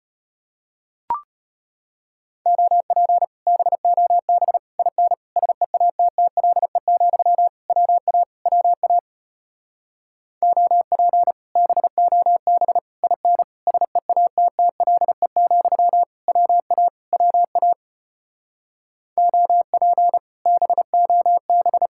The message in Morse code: OP BOB IN SEATTLE, WA WA OP BOB IN SEATTLE, WA WA OP BOB

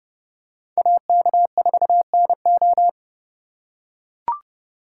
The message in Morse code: AK4NO

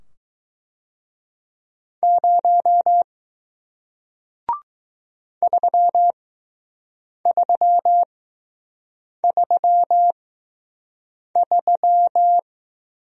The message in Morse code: 0 3333

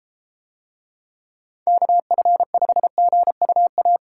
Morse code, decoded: KF5GUA

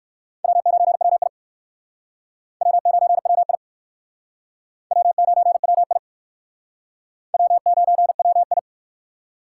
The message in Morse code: W9WI W9WI W9WI W9WI

